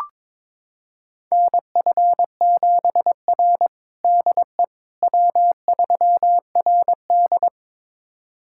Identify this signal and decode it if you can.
Morse code, heard as a keyed tone, about 22 wpm, 700 Hz. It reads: NF7R DE W3RD
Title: USA Callsign DE Callsign - 4 Characters - 22wpm